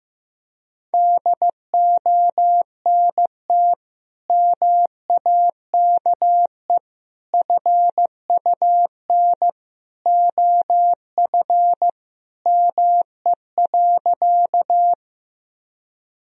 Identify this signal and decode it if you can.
Morse code, heard as a keyed tone, about 15 wpm, 700 Hz. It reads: DONT MAKE FUN OF ME.
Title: Sentences from Top 1000 Words - Review Part 4 - 15wpm